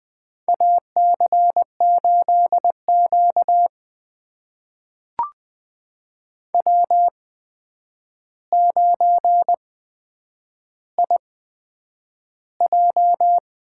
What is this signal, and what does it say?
Morse code, heard as a keyed tone, about 20 wpm, spaced as for 9 wpm, 700 Hz. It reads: AC8Q W9IJ
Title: USA Call Signs - 4 Characters - 8x Character Spacing - 20wpm